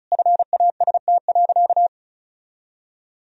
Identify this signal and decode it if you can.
Morse code, heard as a keyed tone, about 34 wpm, 700 Hz. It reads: FAST.
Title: Sentences from Top 1000 Words - Review Part 2 - 35wpm